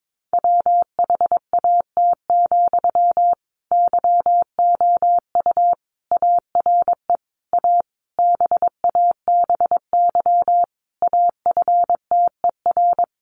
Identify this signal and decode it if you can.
Morse code, heard as a keyed tone, about 22 wpm, 700 Hz. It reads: WHAT, YOU ARE A BABY AFTER
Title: Sentences from Top 1300 Words 22wpm